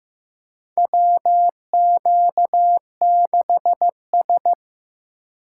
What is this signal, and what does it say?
Morse code, heard as a keyed tone, about 15 wpm, 700 Hz. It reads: WQ6S
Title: USA Call Signs - 4 Characters - 4x Character Spacing - 15wpm